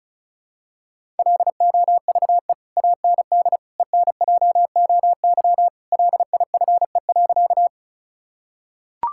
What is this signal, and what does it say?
Morse code, heard as a keyed tone, about 36 wpm, 700 Hz. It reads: LOVE AND ENJOY LIFE.
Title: Sentences from Top 1000 Words - Review Part 5 - 35wpm